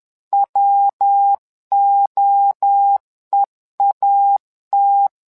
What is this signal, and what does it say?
Morse code, heard as a keyed tone, about 10 wpm, 800 Hz. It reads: WOEAT
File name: LEZIONI / 3 MWRO / 3 MWRO 10wpm 2148